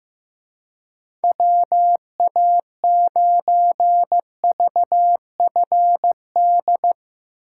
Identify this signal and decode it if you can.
Morse code, heard as a keyed tone, about 15 wpm, 700 Hz. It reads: WA9VFD
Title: Speed Racing - USA 2x3 Call Signs 15wpm